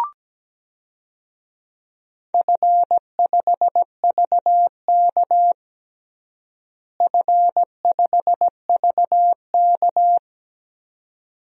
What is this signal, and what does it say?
Morse code, heard as a keyed tone, about 17 wpm, 700 Hz. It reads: F5VK F5VK
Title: DXCC-100 Call Signs - 4 Characters - Repeated 2x - 17wpm